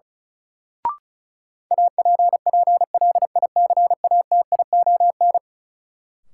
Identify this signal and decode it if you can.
Morse code, heard as a keyed tone, about 36 wpm, 700 Hz. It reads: APPLICATION
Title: Sentences from Top 1200 Words - New Word Drill - 35wpm